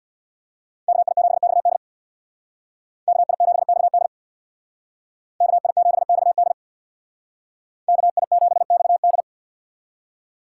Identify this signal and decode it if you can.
Morse code, heard as a keyed tone, about 64 wpm, 700 Hz. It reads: KI7XD KI7XD KI7XD KI7XD